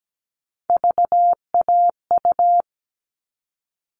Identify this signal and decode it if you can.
Morse code, heard as a keyed tone, about 17 wpm, 700 Hz. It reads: VAU
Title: All Cut Numbers - 3-Digit 17wpm